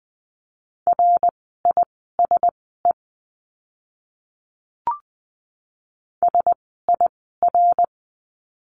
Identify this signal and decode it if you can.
Morse code, heard as a keyed tone, about 20 wpm, spaced as for 15 wpm, 700 Hz. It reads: RISE SIR
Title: 054 Lesson 10f - Letter r - Words 20wpm